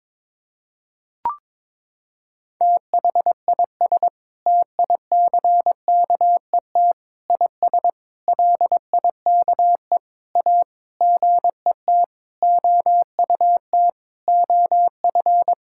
Morse code, heard as a keyed tone, about 22 wpm, 700 Hz. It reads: THIS TICKET IS LIKE A GET OUT OF
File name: Sentences with 7-Word Phrases - 22wpm